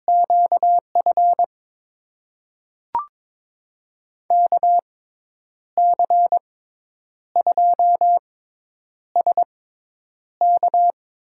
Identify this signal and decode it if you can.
Morse code, heard as a keyed tone, about 22 wpm, spaced as for 9 wpm, 700 Hz. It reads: QF KC2SK
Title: USA Call Signs - 6 Characters - 6x Character Spacing - 22wpm